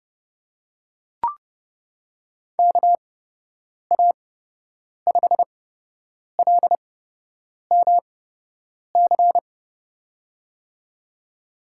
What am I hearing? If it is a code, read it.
Morse code, heard as a keyed tone, about 30 wpm, spaced as for 8 wpm, 700 Hz. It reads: KA5LMC
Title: USA Call Signs - 6 Characters - 8x Character Spacing - 30wpm